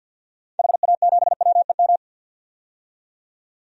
Morse code, heard as a keyed tone, about 52 wpm, 700 Hz. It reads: HA7WEM
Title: DXCC-100 Call Signs - 6 Characters - 50wpm